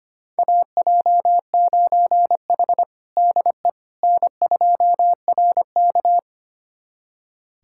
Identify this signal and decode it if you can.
Morse code, heard as a keyed tone, about 25 wpm, 700 Hz. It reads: AJ9H DE N2RK
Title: USA Callsign DE Callsign - 4 Characters - 25wpm